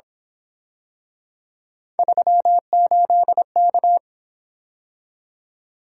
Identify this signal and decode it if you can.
Morse code, heard as a keyed tone, about 28 wpm, 700 Hz. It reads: 38K
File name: Speed-Racing - Three Letter-Number Combinations - 22wpm